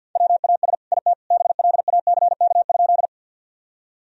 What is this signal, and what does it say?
Morse code, heard as a keyed tone, about 50 wpm, 700 Hz. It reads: WAS IT BLACK?
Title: Sentences from Top 1000 Words - Review Part 2 - 50wpm